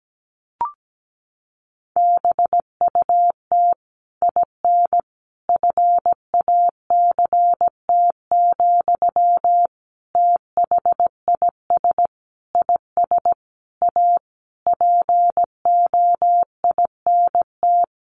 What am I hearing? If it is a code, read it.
Morse code, heard as a keyed tone, about 17 wpm, 700 Hz. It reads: BUT IN FACT, THIS IS A POINT
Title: Sentences from Top 1500 Words 17wpm